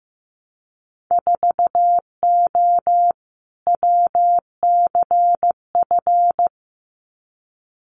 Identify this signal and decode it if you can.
Morse code, heard as a keyed tone, about 15 wpm, 700 Hz. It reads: 4O WCF
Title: Winter Field Day Exchanges Encore 15wpm